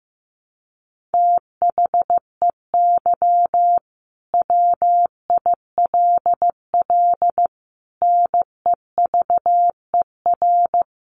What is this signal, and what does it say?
Morse code, heard as a keyed tone, about 15 wpm, 700 Hz. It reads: THEY WILL NEVER
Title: Sentences from Top 1300 Words 15wpm